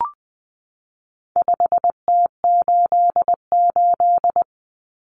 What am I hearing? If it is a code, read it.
Morse code, heard as a keyed tone, about 20 wpm, 700 Hz. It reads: 5T88
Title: Common Cut Numbers - 4-Digit 20wpm